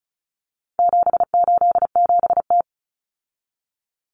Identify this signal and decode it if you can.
Morse code, heard as a keyed tone, about 34 wpm, 700 Hz. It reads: 787T
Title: Common Cut Numbers - 4-Digit 35wpm